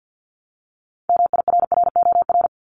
Morse code, heard as a keyed tone, about 50 wpm, 700 Hz. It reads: MIRROR
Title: Multi-letter Phonemes 50wpm